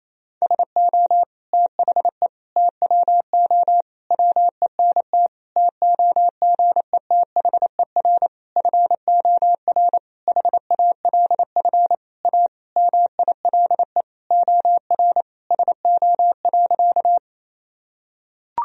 Morse code, heard as a keyed tone, about 27 wpm, 700 Hz. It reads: SO THE TWO WENT TOGETHER FOR HALF A MILE OR SO.
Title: Sentences from Top 1000 Words - Review Part 1 - 28wpm